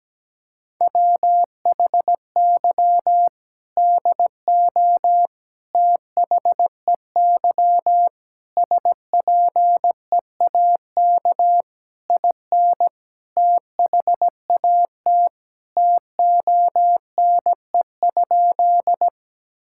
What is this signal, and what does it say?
Morse code, heard as a keyed tone, about 17 wpm, 700 Hz. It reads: WHY DO THEY SPEAK IN THAT TONE?
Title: Sentences from Top 1600 Words 17wpm